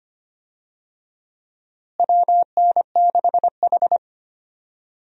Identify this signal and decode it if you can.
Morse code, heard as a keyed tone, about 25 wpm, 700 Hz. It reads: WN6H
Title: USA Call Signs - 4 Characters - Repeated 3x - 25wpm